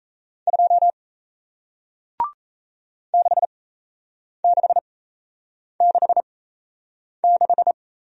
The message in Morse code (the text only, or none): 2 6666